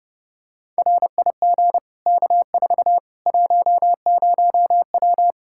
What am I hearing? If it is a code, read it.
Morse code, heard as a keyed tone, about 30 wpm, 700 Hz. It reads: RIG K4 10W
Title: K1USN SST QSO Practice 30wpm